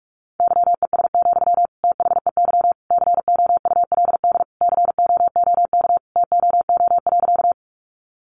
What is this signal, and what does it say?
Morse code, heard as a keyed tone, about 44 wpm, 700 Hz. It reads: YES, THEY COULD COOK TOO.
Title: Sentences from Top 700 Words 45wpm